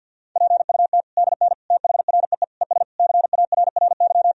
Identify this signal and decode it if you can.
Morse code, heard as a keyed tone, about 49 wpm, 700 Hz. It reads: PUT DN THREE ES CARRY
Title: Sentences from Top 400 Words and Abbreviations 50wpm